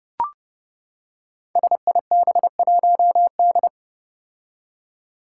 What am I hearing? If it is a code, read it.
Morse code, heard as a keyed tone, about 30 wpm, 700 Hz. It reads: SIB1D
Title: Five Challenging Letter-Number Combinations 30wpm